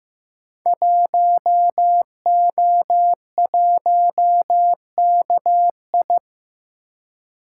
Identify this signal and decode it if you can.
Morse code, heard as a keyed tone, about 15 wpm, 700 Hz. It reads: 1O1KI
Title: Code Groups - 5 Letter:Numbers - 15wpm